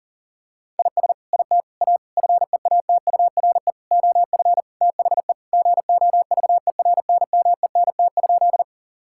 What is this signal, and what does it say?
Morse code, heard as a keyed tone, about 37 wpm, 700 Hz. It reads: IS IT A FEATURE OF THE GOVERNMENT?
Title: Sentences from Top 1000 Words - Review Part 5 - 40wpm